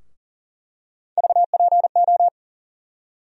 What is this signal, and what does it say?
Morse code, heard as a keyed tone, about 40 wpm, 700 Hz. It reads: VPO
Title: USA Call Sign Postfixes - 3 Letters 40wpm